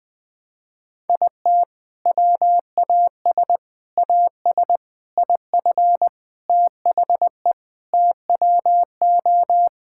Morse code, heard as a keyed tone, about 20 wpm, 700 Hz. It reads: IT WAS AS IF THE TWO
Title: Sentences from Top 1000 Words - Review Part 6 - 20wpm